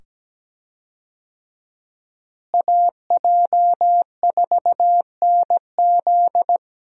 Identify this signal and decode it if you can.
Morse code, heard as a keyed tone, about 17 wpm, 700 Hz. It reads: AJ4NZ
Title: USA Call Signs - 5 Characters - Repeated 2x - 17wpm